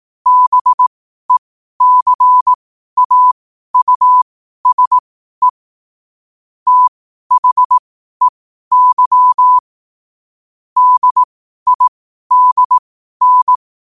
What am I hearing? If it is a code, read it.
Morse code, heard as a keyed tone, about 18 wpm, spaced as for 12 wpm, 1000 Hz. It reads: BECAUSE THEY DIDN
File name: qotdfiles / 12wpm